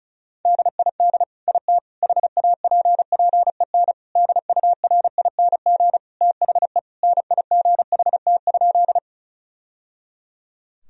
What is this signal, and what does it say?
Morse code, heard as a keyed tone, about 34 wpm, 700 Hz. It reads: DID IT HAPPEN DURING THE NIGHT?
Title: Sentences from Top 1000 Words - Review Part 2 - 35wpm